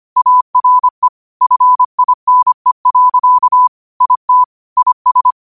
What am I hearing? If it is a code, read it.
Morse code, heard as a keyed tone, about 25 wpm, 1000 Hz. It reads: ARE FINE. IT IS
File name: qotdfiles / 25wpm